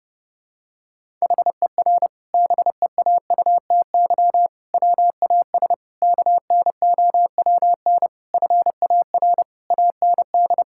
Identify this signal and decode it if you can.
Morse code, heard as a keyed tone, about 30 wpm, 700 Hz. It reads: HER BEAUTY WAS KNOWN FAR AND
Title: Sentences from Top 1000 Words - Review Part 6 - 30wpm